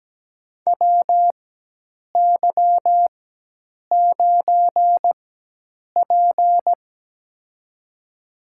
Morse code, heard as a keyed tone, about 17 wpm, spaced as for 8 wpm, 700 Hz. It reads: WY9P
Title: USA Call Signs - 4 Characters - 4x Character Spacing - 17wpm